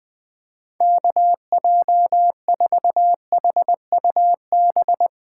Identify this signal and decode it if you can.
Morse code, heard as a keyed tone, about 20 wpm, 700 Hz. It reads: KJ4HUB
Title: USA Call Signs - 6 Characters - 3x Character Spacing - 20wpm